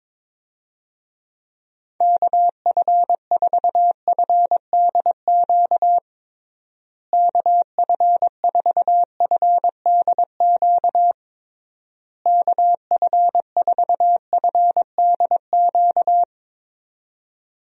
Morse code, heard as a keyed tone, about 22 wpm, 700 Hz. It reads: KF4FDQ KF4FDQ KF4FDQ
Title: USA Call Signs - 6 Characters - Repeated 3x - 22wpm